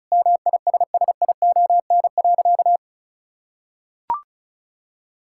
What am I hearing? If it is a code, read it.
Morse code, heard as a keyed tone, about 36 wpm, 700 Hz. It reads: MISSION.
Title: Sentences from Top 1600 Words 35wpm